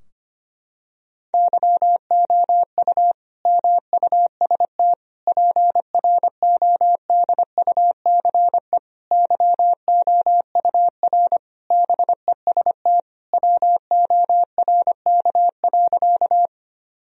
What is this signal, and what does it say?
Morse code, heard as a keyed tone, about 25 wpm, 700 Hz. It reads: YOU MUST PRODUCE YOUR BEST WORK.
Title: Sentences from Top 1000 Words - Review Part 3 - 25wpm